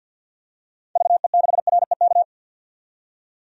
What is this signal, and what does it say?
Morse code, heard as a keyed tone, about 49 wpm, 700 Hz. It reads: VE6REK
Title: DXCC-100 Call Signs - 6 Characters - 50wpm